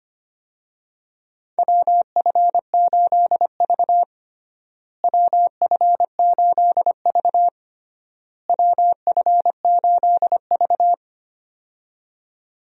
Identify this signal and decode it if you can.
Morse code, heard as a keyed tone, about 25 wpm, 700 Hz. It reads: WF8V WF8V WF8V
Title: USA Call Signs - 4 Characters - Repeated 3x - 25wpm